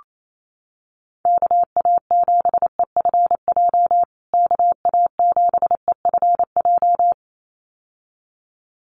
Morse code, heard as a keyed tone, about 28 wpm, 700 Hz. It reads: KA7EFJ KA7EFJ
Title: USA Call Signs Repeated 2x 28wpm